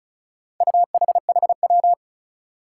Morse code, heard as a keyed tone, about 34 wpm, 700 Hz. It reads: UHHW
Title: Four Challenging Letter-Number Combinations 35wpm